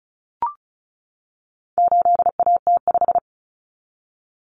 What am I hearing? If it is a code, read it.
Morse code, heard as a keyed tone, about 34 wpm, 700 Hz. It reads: 8AT5
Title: Common Cut Numbers - 4-Digit 35wpm